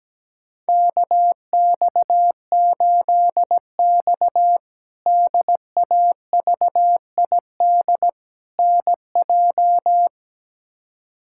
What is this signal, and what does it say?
Morse code, heard as a keyed tone, about 17 wpm, 700 Hz. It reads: KX8X DAVID NJ
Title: K1USN SST QSO Practice - 17wpm